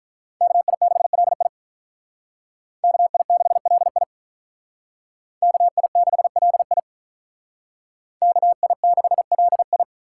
Morse code, heard as a keyed tone, about 43 wpm, 700 Hz. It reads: KI6LI KI6LI KI6LI KI6LI
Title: Speed Racing - USA Call Signs 35wpm